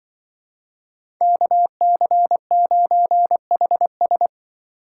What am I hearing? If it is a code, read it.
Morse code, heard as a keyed tone, about 25 wpm, 700 Hz. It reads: KC9HS
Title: Speed Racing - USA 2x2 Call Signs 20wpm